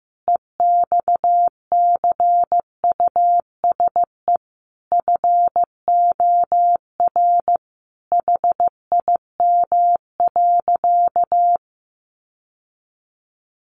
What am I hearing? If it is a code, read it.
Morse code, heard as a keyed tone, about 15 wpm, 700 Hz. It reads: EXCUSE FOR HIM.
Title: Sentences from Top 700 Words 15wpm